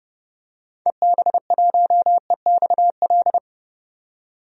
Morse code, heard as a keyed tone, about 30 wpm, 700 Hz. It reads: EB1EXL
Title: DXCC-100 Call Signs - 6 Characters - 30wpm